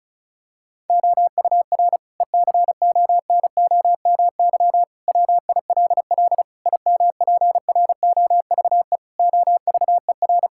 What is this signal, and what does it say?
Morse code, heard as a keyed tone, about 36 wpm, 690 Hz. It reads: OUR ECONOMY WILL IMPROVE OVER
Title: Sentences from Top 1000 Words - Review Part 5 - 35wpm